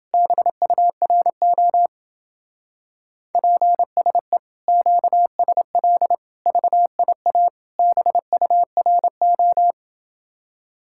Morse code, heard as a keyed tone, about 31 wpm, 700 Hz. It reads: BURO PSE QSL VIA BURO